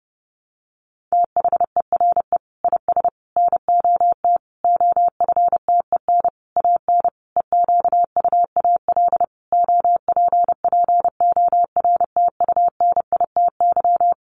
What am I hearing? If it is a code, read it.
Morse code, heard as a keyed tone, about 30 wpm, 700 Hz. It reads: THERE IS NOT OFTEN AN EQUAL OPPORTUNITY
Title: Sentences from Top 1400 Words 30wpm